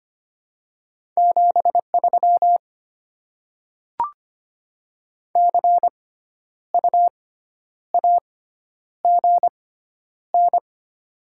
Morse code, heard as a keyed tone, about 25 wpm, spaced as for 10 wpm, 700 Hz. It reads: 73 CU AGN
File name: Top 20 QSO Elements - 6x Character Spacing - 25wpm